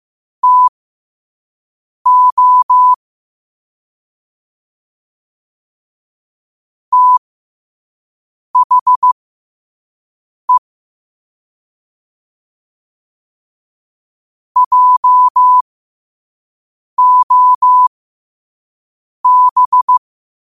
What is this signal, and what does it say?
Morse code, heard as a keyed tone, about 15 wpm, spaced as for 5 wpm, 1000 Hz. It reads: TO THE JOB
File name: qotdfiles / 5wpm